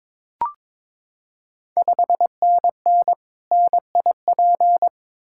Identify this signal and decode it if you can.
Morse code, heard as a keyed tone, about 22 wpm, 700 Hz. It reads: 5NN NIP
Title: Ontario QSO Party Exchange 22wpm